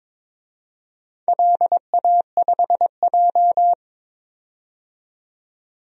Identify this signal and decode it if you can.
Morse code, heard as a keyed tone, about 23 wpm, 700 Hz. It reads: LA5J
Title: DXCC-100 Call Signs - 4 Characters - 22wpm